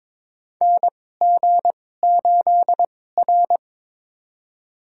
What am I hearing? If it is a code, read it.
Morse code, heard as a keyed tone, about 21 wpm, spaced as for 16 wpm, 700 Hz. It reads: NG8R
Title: USA Call Signs - 4 Characters - 2x Character Spacing - 22wpm